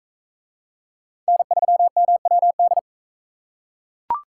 Morse code, heard as a keyed tone, about 41 wpm, 700 Hz. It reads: N3MWD